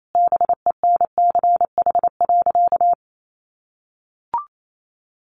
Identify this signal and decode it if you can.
Morse code, heard as a keyed tone, about 28 wpm, 700 Hz. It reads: BENCH.
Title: Sentences from Top 1400 Words 28wpm